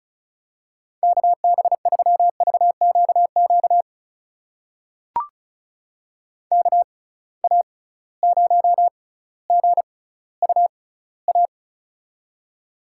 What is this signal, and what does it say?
Morse code, heard as a keyed tone, about 36 wpm, spaced as for 18 wpm, 700 Hz. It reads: KB3VQQ KA0GUA
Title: USA Call Signs - 6 Characters - 6x Character Spacing - 35wpm